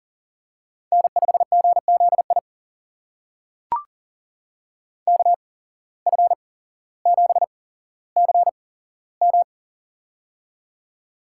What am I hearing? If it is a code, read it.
Morse code, heard as a keyed tone, about 39 wpm, spaced as for 17 wpm, 700 Hz. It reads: N5GZI KF7CM
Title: USA Call Signs - 5 Characters - 8x Character Spacing - 40wpm